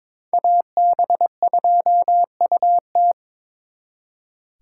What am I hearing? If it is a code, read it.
Morse code, heard as a keyed tone, about 25 wpm, 700 Hz. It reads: AB2UT
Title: Speed Racing - USA 2x2 Call Signs 22wpm